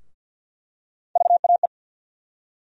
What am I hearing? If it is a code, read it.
Morse code, heard as a keyed tone, about 47 wpm, 700 Hz. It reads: VAE
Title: USA Call Sign Postfixes - 3 Letters 50wpm